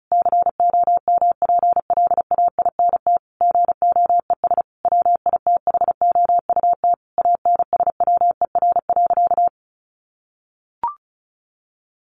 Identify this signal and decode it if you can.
Morse code, heard as a keyed tone, about 34 wpm, 700 Hz. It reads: COMPLAINT GOES WITHOUT ANSWER.
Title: Sentences from Top 1500 Words 35wpm